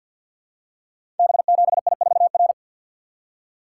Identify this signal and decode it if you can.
Morse code, heard as a keyed tone, about 52 wpm, 700 Hz. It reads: B7I4R
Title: Code Groups - 5 Letter:Numbers - 50wpm